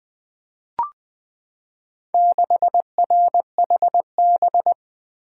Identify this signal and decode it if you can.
Morse code, heard as a keyed tone, about 20 wpm, 700 Hz. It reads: 6RHB
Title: Four Letter-Number Combinations - 20wpm